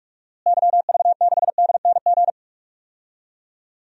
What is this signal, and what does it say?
Morse code, heard as a keyed tone, about 42 wpm, 700 Hz. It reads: YV6DNG
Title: DXCC-100 Call Signs - 6 Characters - Repeated 2x - 45wpm